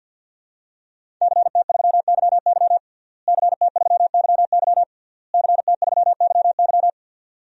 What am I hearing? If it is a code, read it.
Morse code, heard as a keyed tone, about 52 wpm, 700 Hz. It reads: CT3YY CT3YY CT3YY
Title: DXCC-100 Call Signs - 5 Characters - Repeated 3x - 50wpm